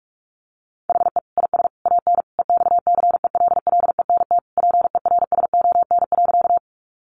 Morse code, heard as a keyed tone, about 44 wpm, 700 Hz. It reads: HE IS AN EXCELLENT PERSON.
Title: Sentences from Top 1000 Words 45wpm